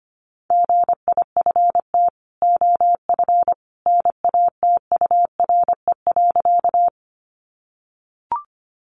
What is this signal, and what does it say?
Morse code, heard as a keyed tone, about 25 wpm, 700 Hz. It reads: GIFT OF NATURE.
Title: Sentences from Top 900 Words 25wpm